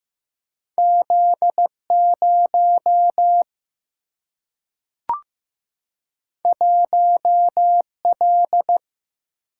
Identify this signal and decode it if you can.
Morse code, heard as a keyed tone, about 15 wpm, 700 Hz. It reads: Z 0 1 L
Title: Single-Letter Single-Number 15wpm